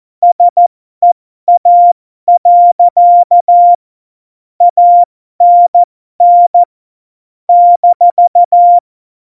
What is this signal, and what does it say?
Morse code, heard as a keyed tone, about 14 wpm, spaced as for 12 wpm, 700 Hz. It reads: SEA. ANN -
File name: Morsekurs / Witze